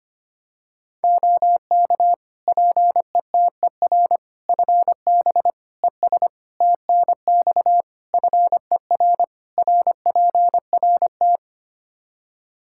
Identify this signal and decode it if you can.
Morse code, heard as a keyed tone, about 25 wpm, 700 Hz. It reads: OK PETER FB ES TNX FER RPRT
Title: K1USN SST QSO Practice 25wpm